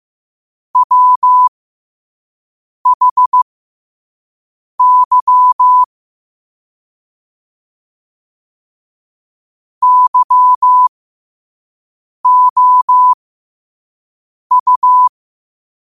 Morse code, heard as a keyed tone, about 15 wpm, spaced as for 5 wpm, 1000 Hz. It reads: WHY YOU
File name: 5wpm